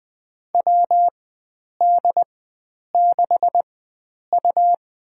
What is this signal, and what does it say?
Morse code, heard as a keyed tone, about 21 wpm, spaced as for 9 wpm, 700 Hz. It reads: WD6U
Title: USA Call Signs - 4 Characters - 4x Character Spacing - 20wpm